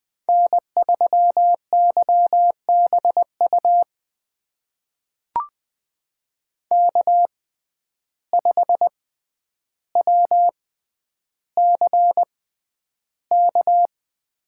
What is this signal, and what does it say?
Morse code, heard as a keyed tone, about 20 wpm, spaced as for 10 wpm, 700 Hz. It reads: N3YBU K5WCK
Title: USA Call Signs - 5 Characters - 6x Character Spacing - 20wpm